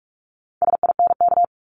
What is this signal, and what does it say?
Morse code, heard as a keyed tone, about 46 wpm, 700 Hz. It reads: SINK
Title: Multi-letter Phonemes 45wpm